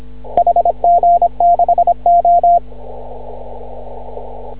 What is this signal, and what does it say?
Morse code, heard as a keyed tone, about 26 wpm, 690 Hz. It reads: HG6O
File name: CW / ha6oi 2